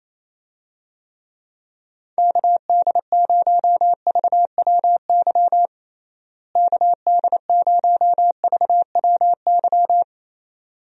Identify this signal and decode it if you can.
Morse code, heard as a keyed tone, about 27 wpm, 700 Hz. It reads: KD0VWY KD0VWY
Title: USA Call Signs - 6 Characters - Repeated 2x - 28wpm